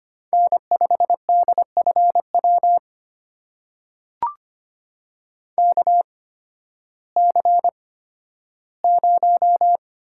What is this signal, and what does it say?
Morse code, heard as a keyed tone, about 25 wpm, spaced as for 13 wpm, 700 Hz. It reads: N5DFW KC0
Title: USA Call Signs - 8x Character Spacing - 25wpm